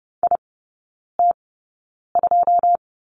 Morse code, heard as a keyed tone, about 30 wpm, spaced as for 9 wpm, 690 Hz. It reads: IT2
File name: Lesson 16e - Letter c - Three Characters 30wpm